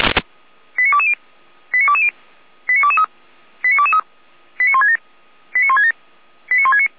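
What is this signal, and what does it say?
Five-tone selective call (ZVEI), heard as quick runs of five tones, tones of about 70 ms. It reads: 89220 89220 89223 89223 89167 89167 89169
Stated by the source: ZVEI selective call received by SDR.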